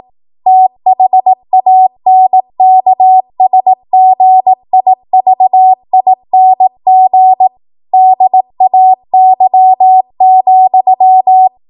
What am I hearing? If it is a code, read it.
Morse code, heard as a keyed tone, about 18 wpm, 750 Hz. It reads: THANKSGIVING DAY,
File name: Archive / 18 WPM / 18WPM